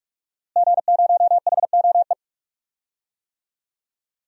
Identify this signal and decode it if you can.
Morse code, heard as a keyed tone, about 41 wpm, 700 Hz. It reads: G0HOE
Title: DXCC-100 Call Signs - 5 Characters - Repeated 3x - 45wpm